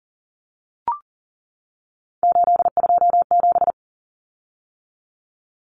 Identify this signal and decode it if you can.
Morse code, heard as a keyed tone, about 40 wpm, 700 Hz. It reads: 827
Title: Common Cut Numbers - 3-Digit 40wpm